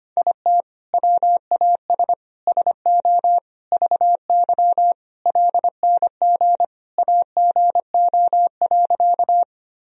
Morse code, heard as a keyed tone, about 25 wpm, 700 Hz. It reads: IT WAS SO VY LNG AGO.
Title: Sentences from Top 500 Words and Abbreviations 25wpm